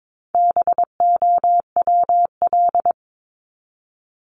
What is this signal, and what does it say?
Morse code, heard as a keyed tone, about 22 wpm, 700 Hz. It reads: BOWL
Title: Top 1001-1250 Words 22wpm